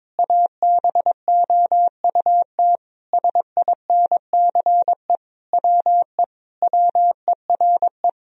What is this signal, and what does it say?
Morse code, heard as a keyed tone, about 22 wpm, 700 Hz. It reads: ABOUT SINCE WE WERE
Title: Sentences from Top 1000 Words - Review Part 1 - 22wpm